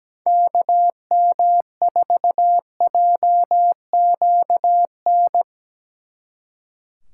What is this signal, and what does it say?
Morse code, heard as a keyed tone, about 19 wpm, 700 Hz. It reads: KM4JQN
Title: Speed Racing - USA 2x3 Call Signs 17wpm